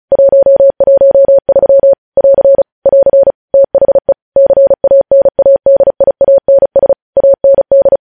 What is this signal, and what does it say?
Morse code, heard as a keyed tone, about 36 wpm, 550 Hz. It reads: 113 <AR> <AR> THE CANADIANS AND
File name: podcast / News headlines-1x-35x35